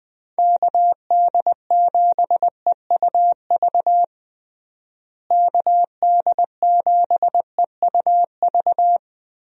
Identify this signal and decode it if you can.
Morse code, heard as a keyed tone, about 21 wpm, 700 Hz. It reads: KD7EUV KD7EUV
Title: USA Call Signs - 6 Characters - Repeated 3x - 20wpm